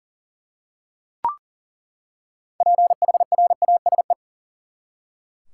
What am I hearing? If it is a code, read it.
Morse code, heard as a keyed tone, about 40 wpm, 700 Hz. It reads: PHRASE
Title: Six Letter Words - 40wpm